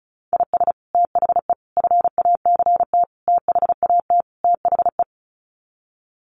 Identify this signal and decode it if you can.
Morse code, heard as a keyed tone, about 34 wpm, 700 Hz. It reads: IS THE FACT THAT THE
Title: Sets of 5 Words Encore2 35wpm